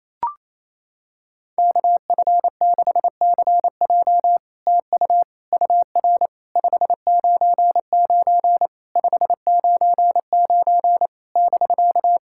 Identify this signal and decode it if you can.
Morse code, heard as a keyed tone, about 28 wpm, 700 Hz. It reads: KF6CJ TU UR 599 599 <BK>
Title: SOTA QSO Practice 28wpm